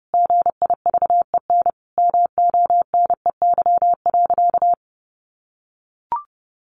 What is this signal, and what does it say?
Morse code, heard as a keyed tone, about 30 wpm, 700 Hz. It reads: GIVEN MONEY.
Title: Sentences from Top 1400 Words 30wpm